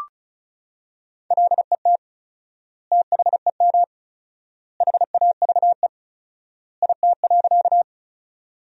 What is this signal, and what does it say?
Morse code, heard as a keyed tone, about 34 wpm, 700 Hz. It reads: LET THEM HAVE IT.
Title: 4x Word-Spacing - Sentences from Top 300 Words - 35wpm